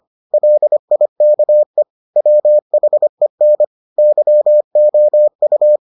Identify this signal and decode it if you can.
Morse code, heard as a keyed tone, about 25 wpm, 600 Hz. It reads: LIKE WHEN YOU